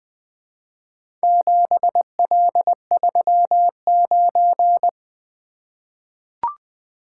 Morse code, heard as a keyed tone, about 20 wpm, 700 Hz. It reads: 7L39
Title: Four Letter-Number Combinations - 20wpm